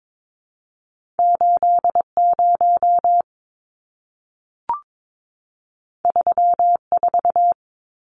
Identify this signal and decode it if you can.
Morse code, heard as a keyed tone, about 22 wpm, 700 Hz. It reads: 80 34
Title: Numbers - 2-Digit 22wpm